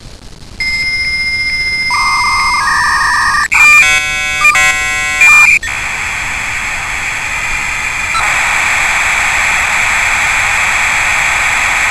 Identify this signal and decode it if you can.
Dial-up modem handshake.